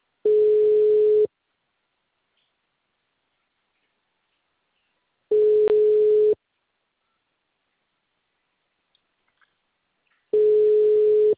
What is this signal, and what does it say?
Ringback tone (Europe, 425 Hz).
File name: out- -2020- - - .37707